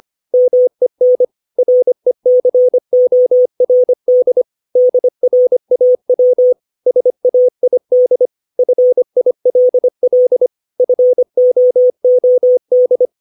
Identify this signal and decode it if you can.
Morse code, heard as a keyed tone, about 25 wpm, 500 Hz. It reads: MEN RECORD DRAW SAID FILL FOOD
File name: 632 words / 632 Words 25WPM